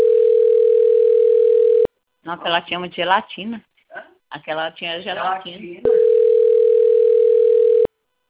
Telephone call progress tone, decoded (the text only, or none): ringback tone